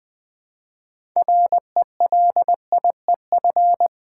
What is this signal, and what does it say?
Morse code, heard as a keyed tone, about 20 wpm, 700 Hz. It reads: RELIEF
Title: Six Letter Words - 20wpm